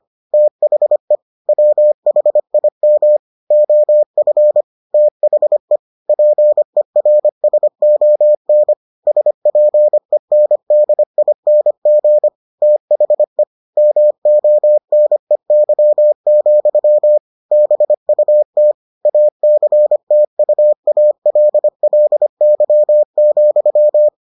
Morse code, heard as a keyed tone, about 25 wpm, 600 Hz. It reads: THE WHIM OF THE PERSON SPENDING THE MONEY, BUT ACTUALLY,